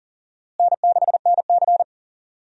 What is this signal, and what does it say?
Morse code, heard as a keyed tone, about 40 wpm, 700 Hz. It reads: N6NC
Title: USA Call Signs - 4 Characters - Repeated 3x - 40wpm